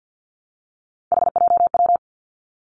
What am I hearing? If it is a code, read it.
Morse code, heard as a keyed tone, about 50 wpm, 700 Hz. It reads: HJW
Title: Three Challenging Letter-Number Combinations 50wpm